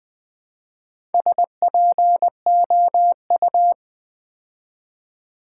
Morse code, heard as a keyed tone, about 20 wpm, 700 Hz. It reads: SPOU
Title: Build A Word - 2 Letters At A Time - 6-Letter Words - 20wpm